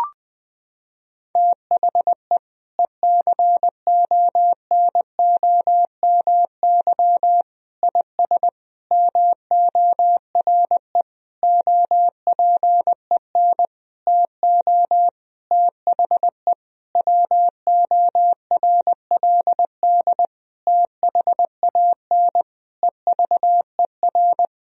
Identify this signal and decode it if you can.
Morse code, heard as a keyed tone, about 21 wpm, 700 Hz. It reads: THE ECONOMY IS MORE OPEN TO THE WORLD THAN EVER
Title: Sentences from Top 1000 Words - Review Part 3 - 20wpm